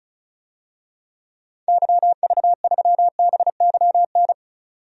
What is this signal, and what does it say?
Morse code, heard as a keyed tone, about 36 wpm, 700 Hz. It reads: YV3BYN
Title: DXCC-100 Call Signs - 6 Characters - 35wpm